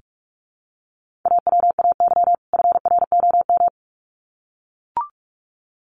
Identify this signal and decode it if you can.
Morse code, heard as a keyed tone, about 44 wpm, 700 Hz. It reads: AWAY FROM
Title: Sets of 2 Words 45wpm